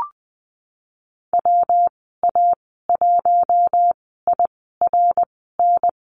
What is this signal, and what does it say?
Morse code, heard as a keyed tone, about 20 wpm, spaced as for 15 wpm, 700 Hz. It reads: WA1IRN
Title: 173 Lesson 28g - Number 9 - Call Signs 20wpm